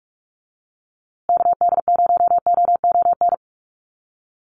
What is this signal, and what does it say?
Morse code, heard as a keyed tone, about 44 wpm, 700 Hz. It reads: KD0OON
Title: USA Call Signs Repeated 2x 45wpm